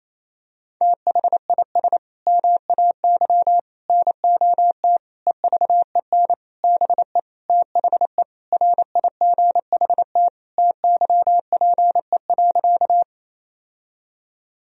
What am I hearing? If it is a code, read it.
Morse code, heard as a keyed tone, about 28 wpm, 700 Hz. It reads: THIS MAY NOT EVEN BE THE RIGHT TYPE.
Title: Sentences from Top 1000 Words - Review Part 3 - 28wpm